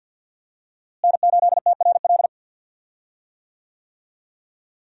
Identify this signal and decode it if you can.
Morse code, heard as a keyed tone, about 49 wpm, 700 Hz. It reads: N8TRL
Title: USA Call Signs - 5 Characters - Repeated 2x - 50wpm